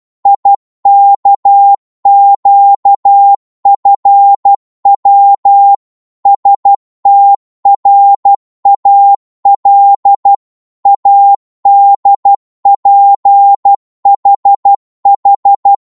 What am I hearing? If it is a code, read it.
Morse code, heard as a keyed tone, about 12 wpm, 800 Hz. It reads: IKQFW STRAL ADPHH
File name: Morse / ltr-12wpm